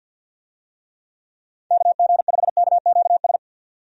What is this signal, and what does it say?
Morse code, heard as a keyed tone, about 48 wpm, 700 Hz. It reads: KG5KQS
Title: USA Call Signs - 6 Characters - 50wpm